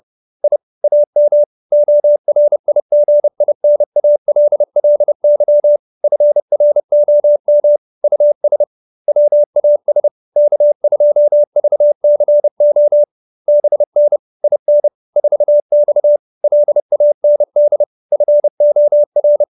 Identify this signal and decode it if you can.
Morse code, heard as a keyed tone, about 30 wpm, 590 Hz. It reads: I AM ORIGINALLY FROM US WAS K2VCO BN IN 4X LAND FOR
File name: QSO315 30